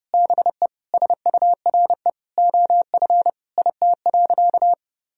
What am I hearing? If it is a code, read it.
Morse code, heard as a keyed tone, about 30 wpm, 700 Hz. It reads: BE SURE OF IT.
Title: Sentences from Top 1000 Words - Review Part 4 - 30wpm